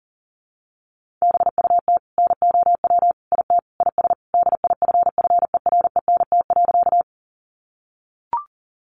Morse code, heard as a keyed tone, about 40 wpm, 700 Hz. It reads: BUT NOW IT IS DIFFERENT.
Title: Sentences from Top 600 Words 40wpm